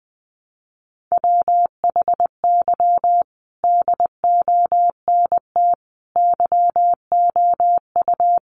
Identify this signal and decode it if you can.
Morse code, heard as a keyed tone, about 20 wpm, 700 Hz. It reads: WHY DONT YOU
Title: Sentences from Top 300 Words 20wpm